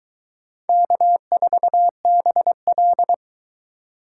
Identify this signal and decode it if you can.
Morse code, heard as a keyed tone, about 21 wpm, 700 Hz. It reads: K4BL